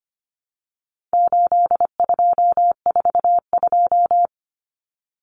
Morse code, heard as a keyed tone, about 25 wpm, 700 Hz. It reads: 8242
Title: Common Cut Numbers - 4-Digit 25wpm